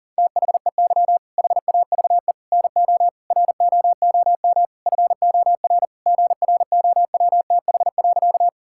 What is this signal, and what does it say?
Morse code, heard as a keyed tone, about 40 wpm, 690 Hz. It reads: THEY HAVE NO ROOM FOR GROWTH.
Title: Sentences from Top 1000 Words - Review Part 5 - 40wpm